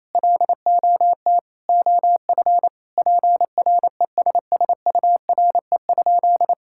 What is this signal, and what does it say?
Morse code, heard as a keyed tone, about 28 wpm, 700 Hz. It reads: LOT OF PRESSURE?
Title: Sentences from Top 1000 Words - Review Part 6 - 28wpm